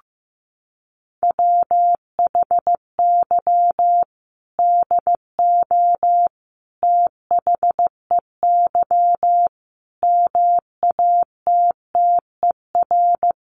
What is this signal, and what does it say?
Morse code, heard as a keyed tone, about 15 wpm, 700 Hz. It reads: WHY DO THEY MATTER
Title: Sentences from Top 700 Words 15wpm